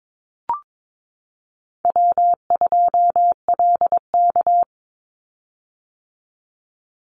Morse code, heard as a keyed tone, about 22 wpm, 700 Hz. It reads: W2LK
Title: Easy Call Signs - 22wpm